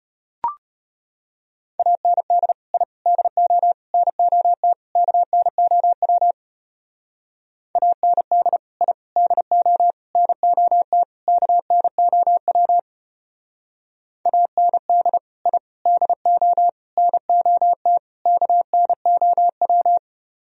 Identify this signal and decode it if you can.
Morse code, heard as a keyed tone, about 32 wpm, 700 Hz. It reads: AND I DO NOT KNOW AND I DO NOT KNOW AND I DO NOT KNOW